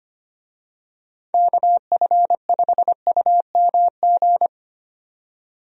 Morse code, heard as a keyed tone, about 25 wpm, 700 Hz. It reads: KF5UMG
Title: USA Call Signs - 6 Characters - 25wpm